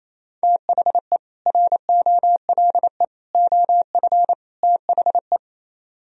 Morse code, heard as a keyed tone, about 34 wpm, 690 Hz. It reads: THE ROLE OF THE
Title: Speed-Racing - Sets of 4 Words 28wpm